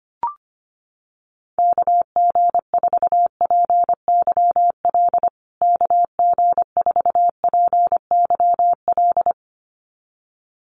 Morse code, heard as a keyed tone, about 25 wpm, 700 Hz. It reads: KG4PYL KG4PYL
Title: USA Call Signs Repeated 2x 25wpm